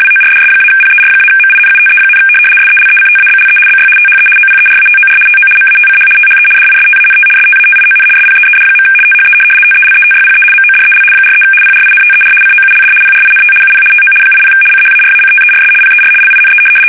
RTTY, about 75 baud. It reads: WORDS-PER-MINUTE, SELECTABLE BY THE OPERATOR TO MATCH THE DISTANT STATION SPEED. THE TELETYPE MODEL 43 PRINTER COULD PRINT UP TO 80 CHARACTERS PER LINE
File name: fsk850-75B / teletype-corp